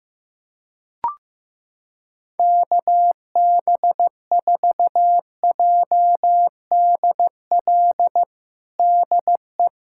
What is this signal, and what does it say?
Morse code, heard as a keyed tone, about 15 wpm, 700 Hz. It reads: KB4JDL DE
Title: USA Callsign DE Callsign 15wpm